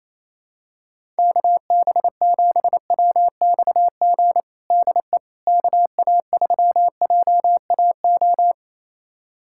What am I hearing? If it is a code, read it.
Morse code, heard as a keyed tone, about 28 wpm, 700 Hz. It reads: KB7WXG DE KA3JAO
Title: USA Callsign DE Callsign 28wpm